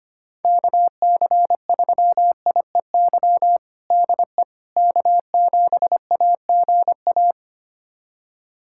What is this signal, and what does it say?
Morse code, heard as a keyed tone, about 25 wpm, 700 Hz. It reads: KC3IEY DE K7AGA
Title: USA Callsign DE Callsign 25wpm